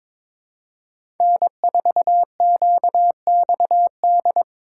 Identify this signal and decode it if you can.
Morse code, heard as a keyed tone, about 22 wpm, 700 Hz. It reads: N4QXD
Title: USA Call Signs - 5 Characters - 4x Character Spacing - 22wpm